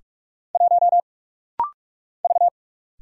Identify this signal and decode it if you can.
Morse code, heard as a keyed tone, about 46 wpm, 700 Hz. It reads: 1 V
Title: Single Letter-Number - Rapid-Fire 45wpm